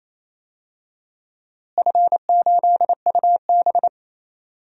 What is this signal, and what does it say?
Morse code, heard as a keyed tone, about 28 wpm, 700 Hz. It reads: F8UB
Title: DXCC-100 Call Signs - 4 Characters - Repeated 2x - 28wpm